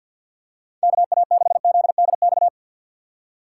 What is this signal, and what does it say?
Morse code, heard as a keyed tone, about 54 wpm, 700 Hz. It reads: KA6ZDX
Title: USA Call Signs - 6 Characters - 6x Character Spacing - 50wpm